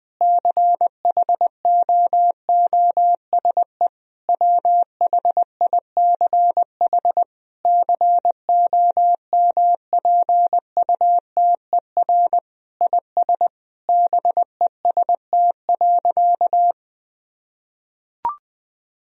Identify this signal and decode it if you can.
Morse code, heard as a keyed tone, about 19 wpm, 700 Hz. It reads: CHOOSE WHICH COMPUTER IS BEST.
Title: Sentences from Top 1000 Words - Review Part 3 - 20wpm